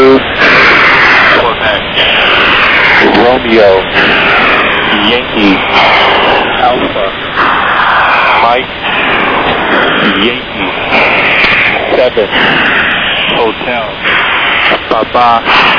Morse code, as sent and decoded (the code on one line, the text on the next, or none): none